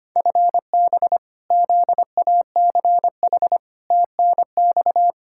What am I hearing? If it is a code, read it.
Morse code, heard as a keyed tone, about 25 wpm, 700 Hz. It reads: FB ZACH TNX
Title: K1USN SST QSO Practice 25wpm